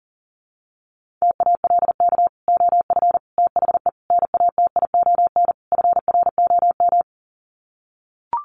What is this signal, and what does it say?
Morse code, heard as a keyed tone, about 40 wpm, 700 Hz. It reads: TALK OF THE NATION FROM
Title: Sets of 5 Words 40wpm